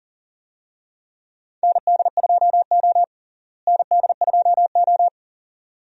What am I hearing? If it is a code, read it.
Morse code, heard as a keyed tone, about 37 wpm, 700 Hz. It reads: ND2O ND2O
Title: USA Call Signs - 4 Characters - Repeated 2x - 40wpm